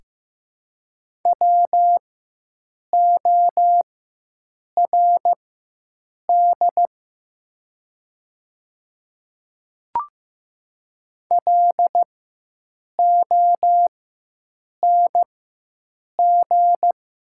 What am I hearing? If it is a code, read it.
Morse code, heard as a keyed tone, about 15 wpm, spaced as for 7 wpm, 700 Hz. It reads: WORD LONG
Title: 4x Character Spacing - Top 100 Words - 15wpm